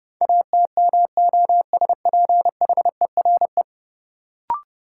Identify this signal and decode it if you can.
Morse code, heard as a keyed tone, about 32 wpm, 700 Hz. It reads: ATMOSPHERE
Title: Sentences from Top 1600 Words - New Word Drill Alt - 30wpm